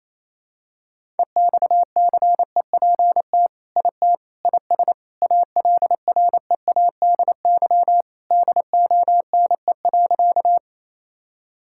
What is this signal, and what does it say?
Morse code, heard as a keyed tone, about 28 wpm, 700 Hz. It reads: EXCEPT IT IS ALREADY DONE.
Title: Sentences from Top 1600 Words 28wpm